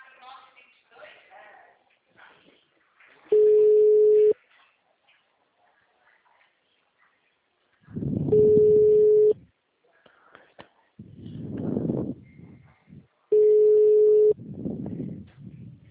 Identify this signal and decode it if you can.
Ringback tone (Europe, 425 Hz), with speech.
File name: out- -2010- - - .24